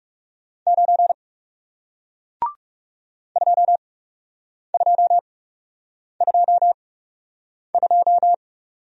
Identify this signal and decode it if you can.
Morse code, heard as a keyed tone, about 40 wpm, spaced as for 8 wpm, 700 Hz. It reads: 9 2222